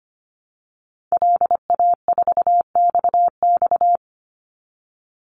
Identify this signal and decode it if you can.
Morse code, heard as a keyed tone, about 25 wpm, 700 Hz. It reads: LA4XX
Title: Call Signs Repeated 3x - 25wpm